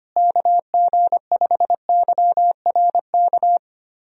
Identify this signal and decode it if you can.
Morse code, heard as a keyed tone, about 25 wpm, 700 Hz. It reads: KG5YRK
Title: USA Call Signs - 6 Characters - Repeated 3x - 25wpm